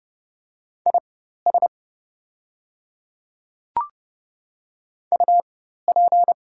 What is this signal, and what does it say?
Morse code, heard as a keyed tone, about 31 wpm, spaced as for 14 wpm, 700 Hz. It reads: IS UP
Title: 4x Character Spacing - Top 100 Words - 30wpm